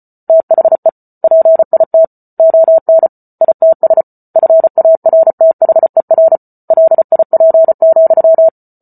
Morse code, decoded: THE PIT ON ITS FARTHER LIP,

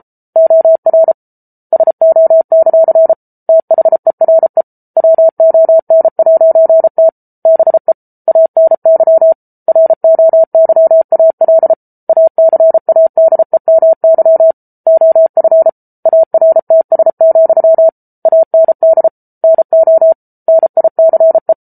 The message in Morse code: OR SO; THERE WON'T BE ANY ROYAL ACADEMY OF ARTS, AND NO NICE

